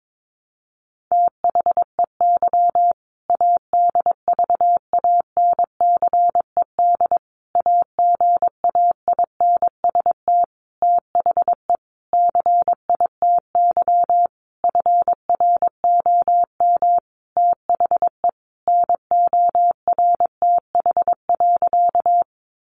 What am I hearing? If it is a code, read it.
Morse code, heard as a keyed tone, about 22 wpm, 700 Hz. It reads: THEY ADVANCED AGAINST THE CITY FROM THE NORTH.
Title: Sentences from Top 1200 Words 22wpm